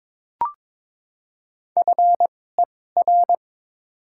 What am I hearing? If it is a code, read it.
Morse code, heard as a keyed tone, about 22 wpm, spaced as for 16 wpm, 690 Hz. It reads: FER
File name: practice / 22wpm / Top 20 QSO Elements - 2x Character Spacing - 22wpm